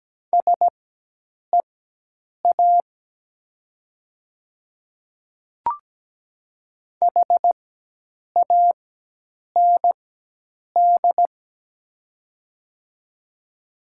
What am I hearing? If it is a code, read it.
Morse code, heard as a keyed tone, about 17 wpm, spaced as for 8 wpm, 700 Hz. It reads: SEA HAND
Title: 4x Character Spacing - Top 300 Words - 17wpm